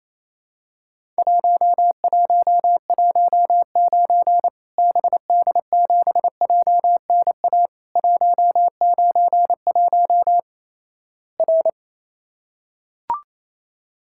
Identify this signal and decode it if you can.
Morse code, heard as a keyed tone, about 28 wpm, 700 Hz. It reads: 1119 BD7JNA 191 R
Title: Worked All Europe DX Contest - QTC Exchange - 28wpm